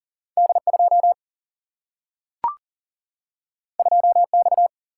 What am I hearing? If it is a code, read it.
Morse code, heard as a keyed tone, about 40 wpm, 700 Hz. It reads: D2 2X
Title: Single-Letter Single-Number 40wpm